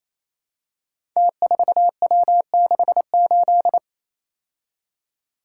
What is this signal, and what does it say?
Morse code, heard as a keyed tone, about 28 wpm, 700 Hz. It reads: T4W68
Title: Code Groups - 5 Letter:Numbers - 28wpm